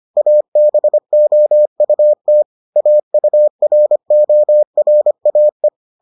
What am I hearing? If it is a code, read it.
Morse code, heard as a keyed tone, about 25 wpm, 600 Hz. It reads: ABOUT AURORAE